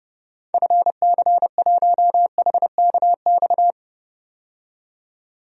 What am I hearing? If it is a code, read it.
Morse code, heard as a keyed tone, about 32 wpm, 700 Hz. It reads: FC1HKX
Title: DXCC-100 Call Signs - 6 Characters - 30wpm